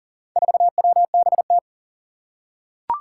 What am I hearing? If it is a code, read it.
Morse code, heard as a keyed tone, about 40 wpm, 700 Hz. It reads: 4WBT
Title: Four Challenging Letter-Number Combinations 40wpm